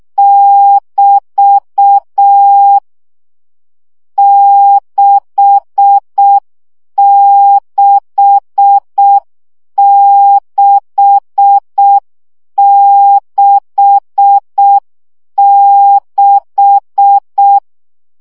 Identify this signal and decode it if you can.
Morse code, heard as a keyed tone, about 6 wpm, 800 Hz. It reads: <BT> 66666
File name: CW gyakorlas 6 5 baud